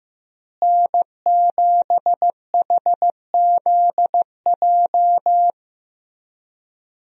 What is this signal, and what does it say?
Morse code, heard as a keyed tone, about 14 wpm, 700 Hz. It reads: N7HZJ
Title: USA Call Signs - 5 Characters - Repeated 3x - 15wpm